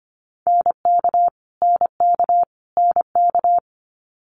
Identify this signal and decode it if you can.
Morse code, heard as a keyed tone, about 25 wpm, 700 Hz. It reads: NK NK NK
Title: Multi-letter Phonemes Repeated 3x 25wpm